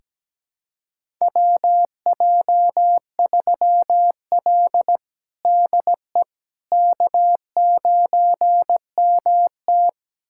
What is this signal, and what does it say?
Morse code, heard as a keyed tone, about 17 wpm, 700 Hz. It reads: WJ3L DE K9MT
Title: USA Callsign DE Callsign - 4 Characters - 17wpm